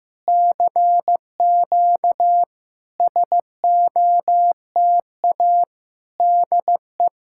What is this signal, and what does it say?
Morse code, heard as a keyed tone, about 15 wpm, 700 Hz. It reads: CQ SOTA DE
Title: SOTA QSO Practice 15wpm